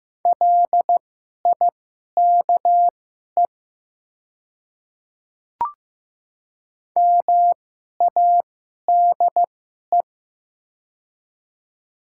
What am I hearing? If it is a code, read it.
Morse code, heard as a keyed tone, about 15 wpm, spaced as for 11 wpm, 700 Hz. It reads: LIKE MADE
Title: 2x Character Spacing - Top 500 Words - 15wpm